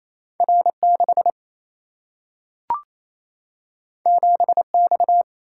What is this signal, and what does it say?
Morse code, heard as a keyed tone, about 28 wpm, 700 Hz. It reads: R6 7X
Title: Single-Letter Single-Number 28wpm